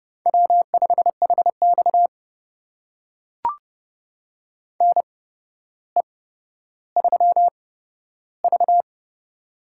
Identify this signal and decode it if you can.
Morse code, heard as a keyed tone, about 30 wpm, spaced as for 13 wpm, 700 Hz. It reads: W5HX NE3V
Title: USA Call Signs - 4 Characters - 8x Character Spacing - 30wpm